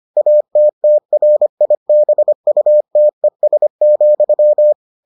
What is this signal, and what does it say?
Morse code, heard as a keyed tone, about 25 wpm, 600 Hz. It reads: ATTRIBUTES,